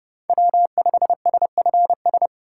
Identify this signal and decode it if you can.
Morse code, heard as a keyed tone, about 32 wpm, 700 Hz. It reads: W5SFS
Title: USA Call Signs - 5 Characters - 3x Character Spacing - 30wpm